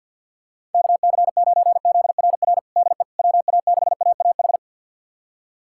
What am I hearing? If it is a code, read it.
Morse code, heard as a keyed tone, about 51 wpm, 700 Hz. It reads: KC9ZRR DE WA6AAH
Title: USA Callsign DE Callsign 50wpm